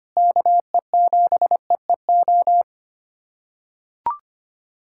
Morse code, heard as a keyed tone, about 25 wpm, 700 Hz. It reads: KE7EEO
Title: USA Callsign DE Callsign 25wpm